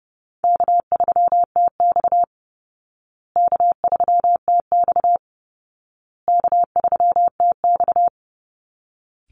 Morse code, heard as a keyed tone, about 30 wpm, 700 Hz. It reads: K3TX K3TX K3TX
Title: Easy Call Signs Repeated 3x - 30wpm